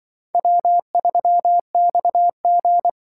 Morse code, heard as a keyed tone, about 24 wpm, 700 Hz. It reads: W3XG
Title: Speed Racing - USA 1x2 Call Signs 20wpm